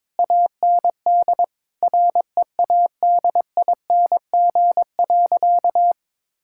Morse code, heard as a keyed tone, about 22 wpm, 700 Hz. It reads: AND READING.
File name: Difficult Sentences with Hints - 22wpm